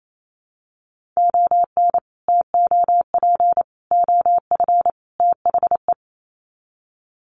Morse code, heard as a keyed tone, about 28 wpm, 700 Hz. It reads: ON TOP OF THE
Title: Sets of 4 Words 28wpm